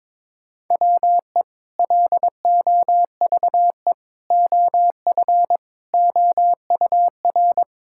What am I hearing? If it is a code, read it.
Morse code, heard as a keyed tone, about 21 wpm, 700 Hz. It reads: WE LOVE OF OUR
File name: practice / 22wpm / Sentences with 2-Word Phrases - 22wpm